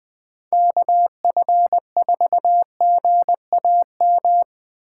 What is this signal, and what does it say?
Morse code, heard as a keyed tone, about 20 wpm, 700 Hz. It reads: KF4GAM
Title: USA Call Signs - 4x Character Spacing - 20wpm